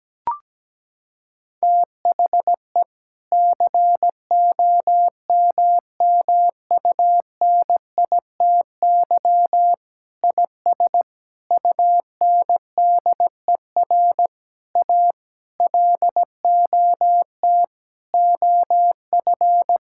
Morse code, heard as a keyed tone, about 17 wpm, 700 Hz. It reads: THE COMMUNITY IS UNDER A LOT OF
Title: Sentences from Top 1000 Words - Review Part 6 - 17wpm